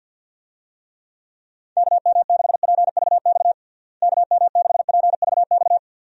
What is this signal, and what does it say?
Morse code, heard as a keyed tone, about 50 wpm, 700 Hz. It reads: KM6PVX KM6PVX
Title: USA Call Signs - 6 Characters - Repeated 3x - 50wpm